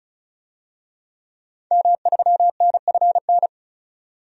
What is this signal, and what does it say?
Morse code, heard as a keyed tone, about 34 wpm, 700 Hz. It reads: M3NFN
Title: DXCC-100 Call Signs - 5 Characters - 35wpm